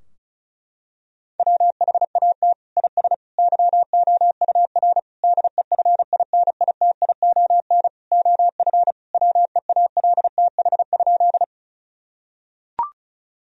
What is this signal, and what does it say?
Morse code, heard as a keyed tone, about 36 wpm, 700 Hz. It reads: WHAT IS YOUR DEFINITION OF WEALTH?
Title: Sentences from Top 1000 Words - Review Part 4 - 35wpm